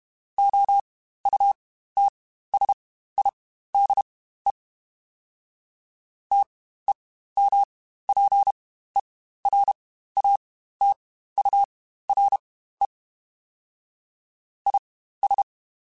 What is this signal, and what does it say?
Morse code, heard as a keyed tone, about 32 wpm, spaced as for 15 wpm, 800 Hz. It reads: OUTSIDE TEMPERATURE IS
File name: Spaced Characters / Spaced Characters-mp3s / Weather 5 12 sp @30